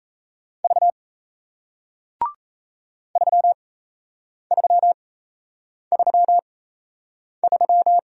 Morse code, heard as a keyed tone, about 37 wpm, spaced as for 8 wpm, 700 Hz. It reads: V 3333